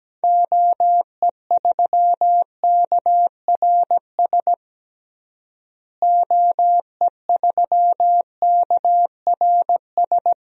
Morse code, heard as a keyed tone, about 17 wpm, 700 Hz. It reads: OE3KRS OE3KRS
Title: DXCC-100 Call Signs - 6 Characters - Repeated 2x - 17wpm